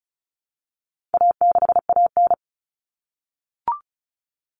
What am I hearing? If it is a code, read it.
Morse code, heard as a keyed tone, about 34 wpm, 700 Hz. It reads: A6AN
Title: Common Cut Numbers - 4-Digit 35wpm